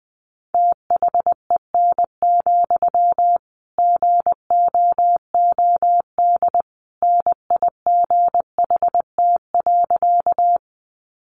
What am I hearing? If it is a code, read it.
Morse code, heard as a keyed tone, about 20 wpm, 700 Hz. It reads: THEN, GOOD NIGHT.
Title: Sentences from Top 300 Words 20wpm